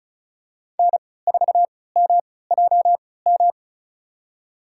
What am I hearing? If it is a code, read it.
Morse code, heard as a keyed tone, about 34 wpm, spaced as for 20 wpm, 700 Hz. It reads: N4MJM
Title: USA Call Signs - 5 Characters - 3x Character Spacing - 35wpm